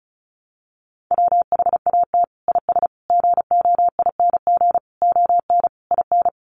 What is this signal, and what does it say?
Morse code, heard as a keyed tone, about 34 wpm, 700 Hz. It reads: WHAT IS GOING ON IN
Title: Sets of 5 Words Encore2 35wpm